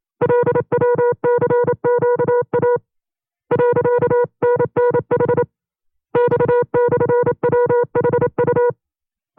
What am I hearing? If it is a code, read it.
Morse code, heard as a keyed tone, about 27 wpm, 480 Hz. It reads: LWCQA .NNH X/WHU
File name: FreeFiles / CW-practice / 25WPM-side1